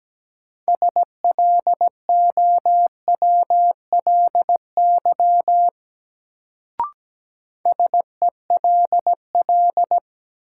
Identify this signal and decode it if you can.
Morse code, heard as a keyed tone, about 17 wpm, 700 Hz. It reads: SLOWLY SELL
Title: Sentences from Top 1100 Words - New Word Drill Alt - 17wpm